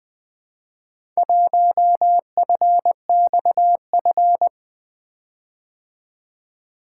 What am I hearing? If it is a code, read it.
Morse code, heard as a keyed tone, about 20 wpm, 700 Hz. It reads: 1FXF
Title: Four Letter-Number Combinations - 20wpm